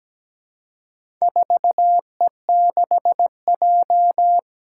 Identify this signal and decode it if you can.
Morse code, heard as a keyed tone, about 17 wpm, 700 Hz. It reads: 4E6J
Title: Four Challenging Letter-Number Combinations 17wpm